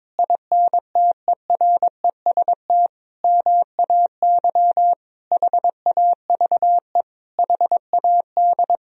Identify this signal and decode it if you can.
Morse code, heard as a keyed tone, about 22 wpm, 700 Hz. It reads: INTEREST MAY HAVE HAD
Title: Sentences from Top 1000 Words - Review Part 2 - 22wpm